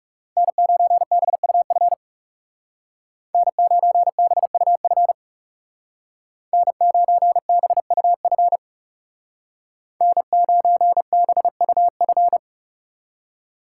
Morse code, heard as a keyed tone, about 37 wpm, 700 Hz. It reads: N9BUF N9BUF N9BUF N9BUF
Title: Speed Racing - USA 1x3 Call Signs 30wpm